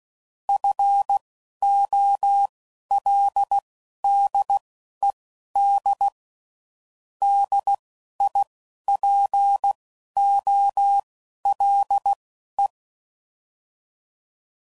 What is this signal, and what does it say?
Morse code, heard as a keyed tone, about 16 wpm, spaced as for 11 wpm, 800 Hz. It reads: FOLDED DIPOLE
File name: CW Code Groups @10WPM / fromSM's / Antennas